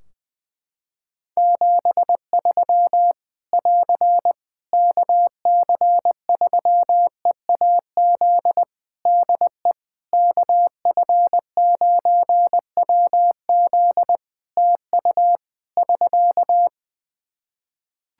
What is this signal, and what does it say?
Morse code, heard as a keyed tone, about 20 wpm, 700 Hz. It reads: 73 <AR> KC3EAZ DE KF9WZ TU <SK>
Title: Practice QSO - Exchange 4 - Ending - 20wpm